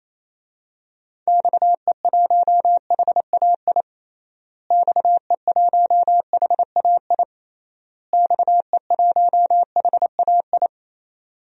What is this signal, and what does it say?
Morse code, heard as a keyed tone, about 28 wpm, 700 Hz. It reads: XE1HAI XE1HAI XE1HAI
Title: DXCC-100 Call Signs - 6 Characters - Repeated 3x - 28wpm